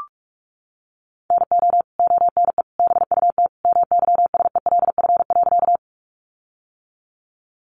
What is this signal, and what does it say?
Morse code, heard as a keyed tone, about 44 wpm, 700 Hz. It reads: NO ONE BUT MYSELF.
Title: Sentences from Top 1300 Words 45wpm